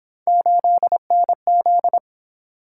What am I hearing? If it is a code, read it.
Morse code, heard as a keyed tone, about 28 wpm, 700 Hz. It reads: 8NZ